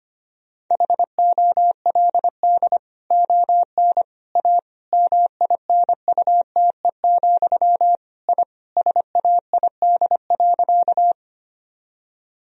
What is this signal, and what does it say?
Morse code, heard as a keyed tone, about 25 wpm, 700 Hz. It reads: HOLD ON A MINUTE, I SAID.
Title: Sentences from Top 1000 Words - Review Part 2 - 25wpm